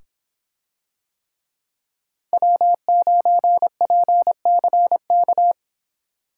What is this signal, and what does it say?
Morse code, heard as a keyed tone, about 26 wpm, 700 Hz. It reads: W9PCK